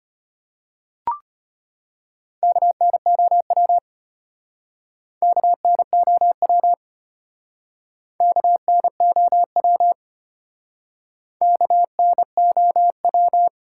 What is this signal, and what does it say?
Morse code, heard as a keyed tone, about 31 wpm, 700 Hz. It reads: KNOW KNOW KNOW KNOW
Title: Speed-Racing Top 180 QSO Elements 25wpm